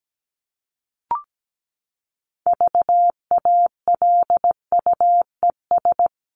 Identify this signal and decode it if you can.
Morse code, heard as a keyed tone, about 17 wpm, 700 Hz. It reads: VALUES
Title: ES:S Suffix - Top 100 Words with Base Words 17wpm